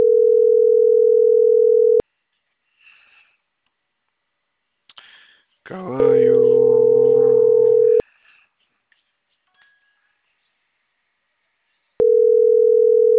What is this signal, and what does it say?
Ringback tone (North America), 440 + 480 Hz.